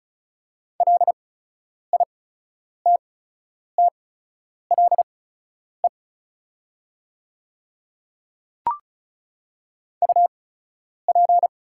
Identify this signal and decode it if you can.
Morse code, heard as a keyed tone, about 36 wpm, spaced as for 10 wpm, 700 Hz. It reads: LITTLE UP
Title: 8x Character Spacing - Top 200 Words - 35wpm